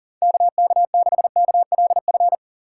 Morse code, heard as a keyed tone, about 40 wpm, 700 Hz. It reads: KK6KLF
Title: USA Call Signs - 6 Characters - 4x Character Spacing - 40wpm